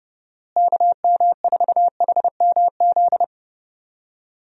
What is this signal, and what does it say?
Morse code, heard as a keyed tone, about 30 wpm, 700 Hz. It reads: KM4HMZ
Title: USA Call Signs - 6 Characters - Repeated 3x - 30wpm